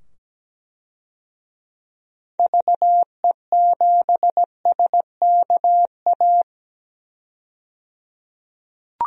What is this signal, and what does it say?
Morse code, heard as a keyed tone, about 17 wpm, 700 Hz. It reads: VE7SKA
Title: DXCC-100 Call Signs - 6 Characters - 17wpm